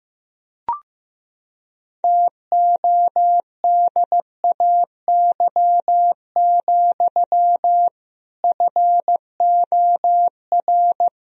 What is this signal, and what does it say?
Morse code, heard as a keyed tone, about 15 wpm, 700 Hz. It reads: TODAY, FOR
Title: Sentences from Top 1000 Words - Review Part 1 - 15wpm